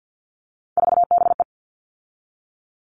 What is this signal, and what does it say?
Morse code, heard as a keyed tone, about 50 wpm, 700 Hz. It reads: 4BE
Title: All Cut Numbers - 3-Digit 50wpm